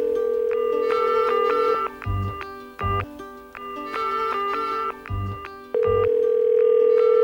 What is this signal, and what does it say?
Ringback tone (North America), 440 + 480 Hz.